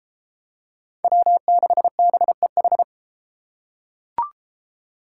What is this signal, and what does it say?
Morse code, heard as a keyed tone, about 31 wpm, 700 Hz. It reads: W6BEH